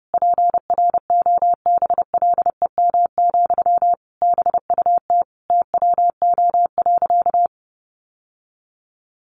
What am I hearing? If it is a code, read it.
Morse code, heard as a keyed tone, about 30 wpm, 700 Hz. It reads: PROBLEM, BUT TWO.
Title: Sentences from Top 1400 Words 30wpm